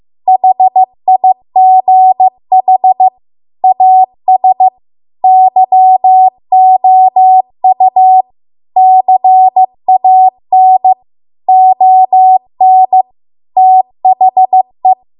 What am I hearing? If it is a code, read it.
Morse code, heard as a keyed tone, about 15 wpm, 750 Hz. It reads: HIGH AS YOU CAN ON THE
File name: Archive / 15 WPM / 15WPM